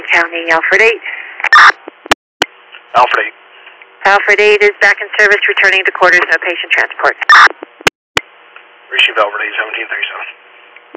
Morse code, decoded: none

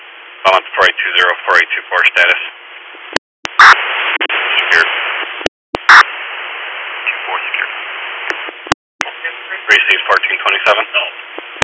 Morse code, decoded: none

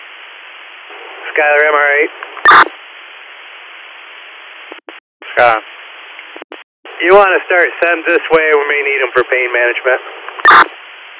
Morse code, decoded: none

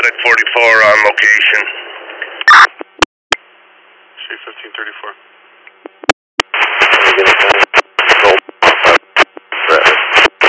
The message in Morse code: none

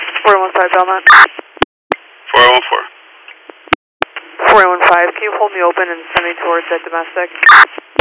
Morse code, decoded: none

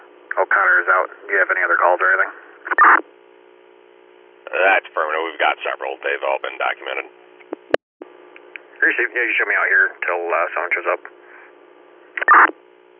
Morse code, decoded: none

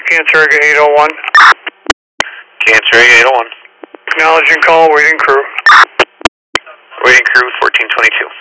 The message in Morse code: none

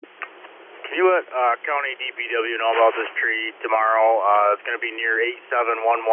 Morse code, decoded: none

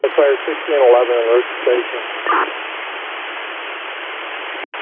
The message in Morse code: none

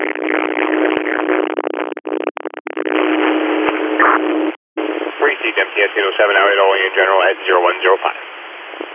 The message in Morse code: none

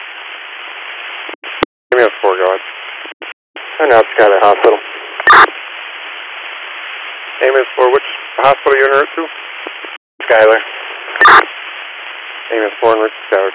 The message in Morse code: none

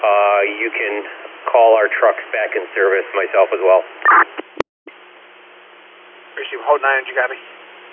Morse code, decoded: none

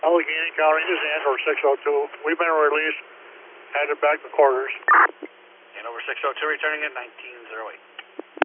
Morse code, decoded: none

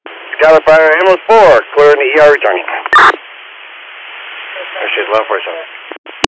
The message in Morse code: none

